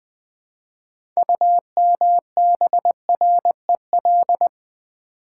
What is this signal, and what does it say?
Morse code, heard as a keyed tone, about 20 wpm, 690 Hz. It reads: UMBREL
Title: Build A Word - 3 Letters At A Time - 8-Letter Words - 20wpm